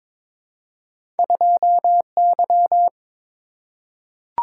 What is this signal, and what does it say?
Morse code, heard as a keyed tone, about 22 wpm, 700 Hz. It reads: 2Y
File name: Two Letter-Number Combinations - 22wpm